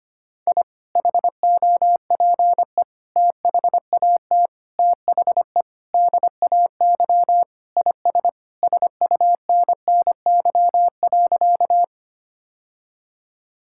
Morse code, heard as a keyed tone, about 25 wpm, 700 Hz. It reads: I HOPE THAT THE DAY IS SUNNY.
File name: practice / 25wpm / Sentences with 2-Word Phrases - 25wpm